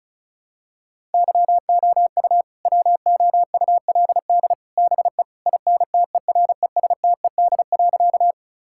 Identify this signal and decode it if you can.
Morse code, heard as a keyed tone, about 36 wpm, 700 Hz. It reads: YOU WOULD BE INTERESTED.
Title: Sentences from Top 1000 Words - Review Part 6 - 35wpm